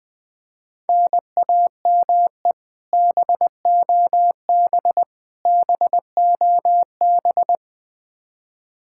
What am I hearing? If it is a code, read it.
Morse code, heard as a keyed tone, about 21 wpm, 700 Hz. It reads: NAME BOB BOB
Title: K1USN SST QSO Practice 20wpm